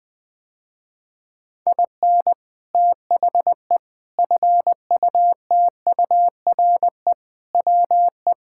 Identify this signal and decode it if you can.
Morse code, heard as a keyed tone, about 19 wpm, 700 Hz. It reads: IN THE FUTURE WE
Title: Sentences with 3-Word Phrases - Spoken Beginning - 20wpm